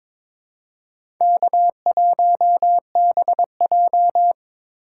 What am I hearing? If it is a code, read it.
Morse code, heard as a keyed tone, about 22 wpm, 700 Hz. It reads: K1BJ
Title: USA Call Signs - 4 Characters - 4x Character Spacing - 22wpm